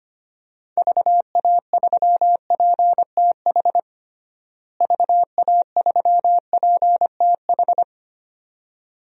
Morse code, heard as a keyed tone, about 25 wpm, 700 Hz. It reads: VA3PTH VA3PTH
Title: DXCC-100 Call Signs - 6 Characters - Repeated 2x - 25wpm